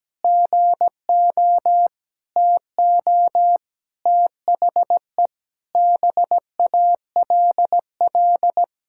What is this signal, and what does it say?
Morse code, heard as a keyed tone, about 17 wpm, 700 Hz. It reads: GO TO THE BALL
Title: Sentences from Top 1000 Words - Review Part 6 - 17wpm